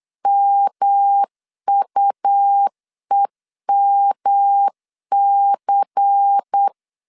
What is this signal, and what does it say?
Morse code, heard as a keyed tone, about 8 wpm, 800 Hz. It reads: MUEMC
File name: LEZIONI / 6 PKX0 / 6 P 8wpm 3025